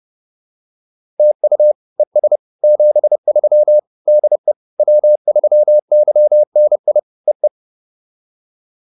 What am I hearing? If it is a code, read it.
Morse code, heard as a keyed tone, about 30 wpm, 600 Hz. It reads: TU ES 73 DE W3YNI E E
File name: POTA101 30